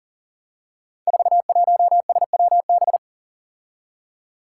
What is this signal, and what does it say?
Morse code, heard as a keyed tone, about 40 wpm, 700 Hz. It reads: 41SWB
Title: Five Challenging Letter-Number Combinations 40wpm